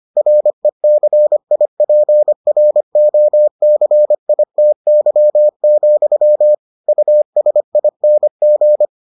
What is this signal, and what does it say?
Morse code, heard as a keyed tone, about 25 wpm, 600 Hz. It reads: RECIPROCITY, USING